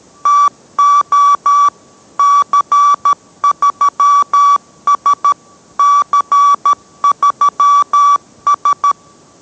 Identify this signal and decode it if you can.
Morse code, heard as a keyed tone, about 13 wpm, 1150 Hz. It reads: TO C3S C3S